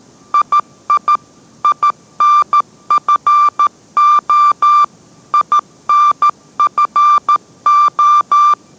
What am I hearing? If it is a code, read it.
Morse code, heard as a keyed tone, about 13 wpm, 1180 Hz. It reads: II INFO INFO